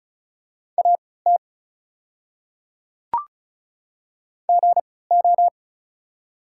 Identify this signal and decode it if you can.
Morse code, heard as a keyed tone, about 36 wpm, spaced as for 20 wpm, 690 Hz. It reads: AT GO
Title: 3x Character Spacing - Top 100 Words - 35wpm